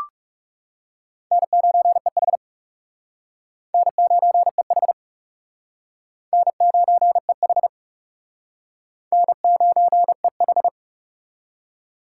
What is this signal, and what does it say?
Morse code, heard as a keyed tone, about 35 wpm, 700 Hz. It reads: N9EH N9EH N9EH N9EH
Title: Speed Racing - USA 1x2 Call Signs 30wpm